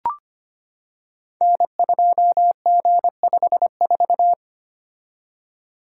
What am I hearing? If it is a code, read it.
Morse code, heard as a keyed tone, about 25 wpm, 700 Hz. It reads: N2G54
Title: Code Groups - 5 Letter:Numbers - 25wpm